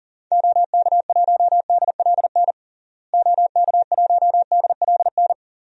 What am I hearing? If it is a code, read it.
Morse code, heard as a keyed tone, about 40 wpm, 700 Hz. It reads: OK1DLN OK1DLN
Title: DXCC-100 Call Signs - 6 Characters - Repeated 3x - 40wpm